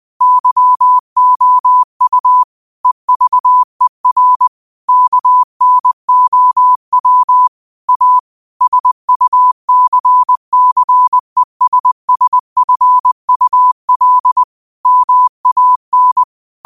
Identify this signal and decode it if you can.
Morse code, heard as a keyed tone, about 20 wpm, 1000 Hz. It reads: YOU EVER KNOW A SUCCESSFUL MAN
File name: qotdfiles / 20wpm